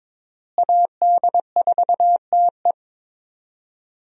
Morse code, heard as a keyed tone, about 22 wpm, 700 Hz. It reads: AD4TE
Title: USA Call Signs - 5 Characters - Repeated 2x - 22wpm